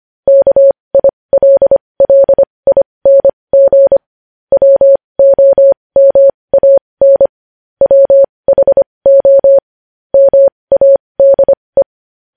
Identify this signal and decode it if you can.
Morse code, heard as a keyed tone, about 25 wpm, spaced as for 20 wpm, 550 Hz. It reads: KILLING WOMAN WHO MADE
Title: News Headlines at 20